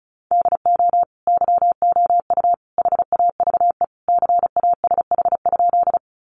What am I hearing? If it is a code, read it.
Morse code, heard as a keyed tone, about 34 wpm, 700 Hz. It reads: DO YOU HAVE CASH?
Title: Sentences from Top 700 Words 35wpm